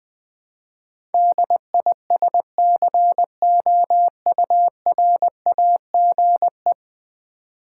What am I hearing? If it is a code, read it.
Morse code, heard as a keyed tone, about 20 wpm, 700 Hz. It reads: DISCOURAGE
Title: Ten Letter Words - 20wpm